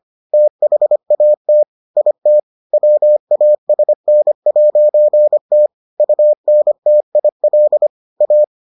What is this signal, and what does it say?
Morse code, heard as a keyed tone, about 25 wpm, 600 Hz. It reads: THAT IT WASN'T UNTIL A